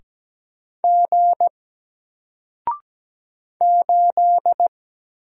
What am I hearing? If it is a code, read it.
Morse code, heard as a keyed tone, about 17 wpm, 700 Hz. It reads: G 8
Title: Single Letter-Number Alt 17wpm